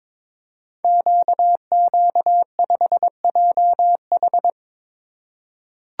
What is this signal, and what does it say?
Morse code, heard as a keyed tone, about 22 wpm, 700 Hz. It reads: QQ5JH
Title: Code Groups - 5 Letter:Numbers - 22wpm